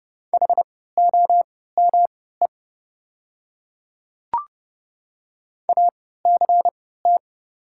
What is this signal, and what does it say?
Morse code, heard as a keyed tone, about 30 wpm, spaced as for 17 wpm, 700 Hz. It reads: HOME ACT
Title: 3x Character Spacing - Top 200 Words - 30wpm